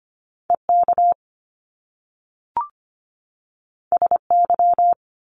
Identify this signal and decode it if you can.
Morse code, heard as a keyed tone, about 25 wpm, 700 Hz. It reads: EK SY
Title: Top Two Letter Combinations in Words 25wpm